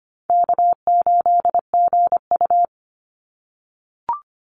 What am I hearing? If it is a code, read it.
Morse code, heard as a keyed tone, about 25 wpm, 700 Hz. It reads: K8GU
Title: Call Signs - 25wpm